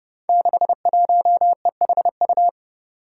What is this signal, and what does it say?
Morse code, heard as a keyed tone, about 30 wpm, 700 Hz. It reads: 61EHU
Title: Five Challenging Letter-Number Combinations 30wpm